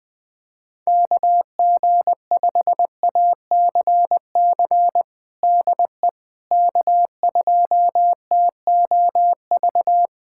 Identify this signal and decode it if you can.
Morse code, heard as a keyed tone, about 21 wpm, 700 Hz. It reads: KG5ACC DE K2TOV
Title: USA Callsign DE Callsign 20wpm